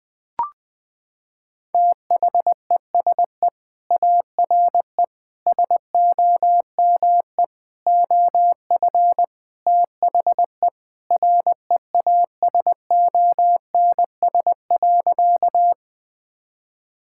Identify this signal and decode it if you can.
Morse code, heard as a keyed tone, about 20 wpm, 700 Hz. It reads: THESE ARE SOME OF THE REASONS.
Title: Sentences with 4-Word Phrases - 20wpm